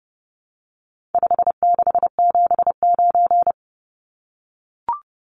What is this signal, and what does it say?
Morse code, heard as a keyed tone, about 30 wpm, 700 Hz. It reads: 5679
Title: Numbers - 4-Digit 30wpm